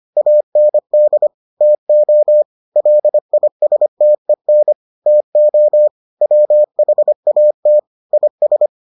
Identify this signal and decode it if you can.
Morse code, heard as a keyed tone, about 25 wpm, 600 Hz. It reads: AND TO LISTEN TO WHAT IS